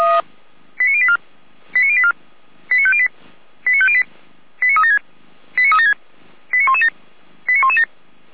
Five-tone selective call (ZVEI), heard as quick runs of five tones, tones of about 70 ms. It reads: 89084 89084 89508 89508 89376 89376 89117 89117